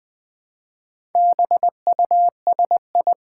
Morse code, heard as a keyed tone, about 20 wpm, 700 Hz. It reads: BUSI
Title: Build A Word - 2 Letters At A Time - 8-Letter Words - 20wpm